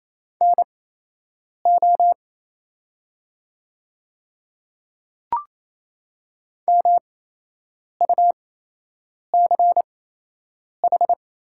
Morse code, heard as a keyed tone, about 27 wpm, spaced as for 8 wpm, 700 Hz. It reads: NO MUCH
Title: 8x Character Spacing - Top 200 Words - 28wpm